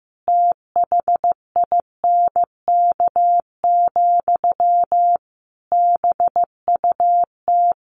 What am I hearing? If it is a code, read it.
Morse code, heard as a keyed tone, about 15 wpm, 700 Hz. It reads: THINK, BUT
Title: A Sound Of Thunder 15wpm